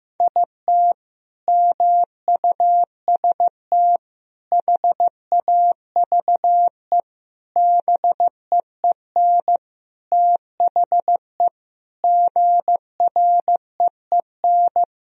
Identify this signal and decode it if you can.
Morse code, heard as a keyed tone, about 15 wpm, 700 Hz. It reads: IT MUST HAVE BEEN THE GREEN
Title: Sentences from Top 1000 Words - Review Part 2 - 15wpm